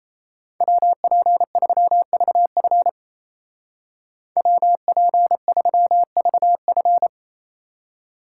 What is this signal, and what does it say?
Morse code, heard as a keyed tone, about 35 wpm, 700 Hz. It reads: WP3VF WP3VF
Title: Speed Racing - USA 2x2 Call Signs 28wpm